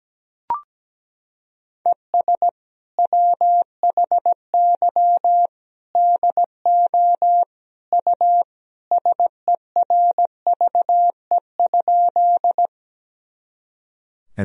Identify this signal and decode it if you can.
Morse code, heard as a keyed tone, about 17 wpm, 700 Hz. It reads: ES WHY DO U SERVE?
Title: Sentences from Top 400 Words and Abbreviations 17wpm